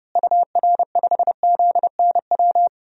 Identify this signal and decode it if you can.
Morse code, heard as a keyed tone, about 32 wpm, 700 Hz. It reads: UR5ZNW
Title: DXCC-100 Call Signs - 6 Characters - Repeated 3x - 30wpm